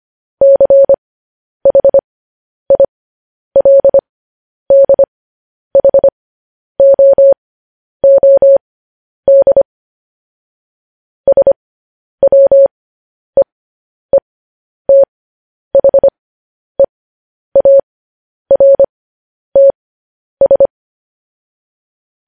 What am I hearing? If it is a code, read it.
Morse code, heard as a keyed tone, about 25 wpm, spaced as for 10 wpm, 550 Hz. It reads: CHILDHOOD SWEETHEARTS
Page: news-headlines-in-morse-code-at-10-wpm